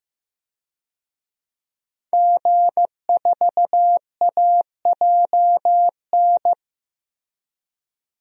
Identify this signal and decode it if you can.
Morse code, heard as a keyed tone, about 15 wpm, 700 Hz. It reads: G4AJN
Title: DXCC-100 Call Signs - 5 Characters - Repeated 2x - 15wpm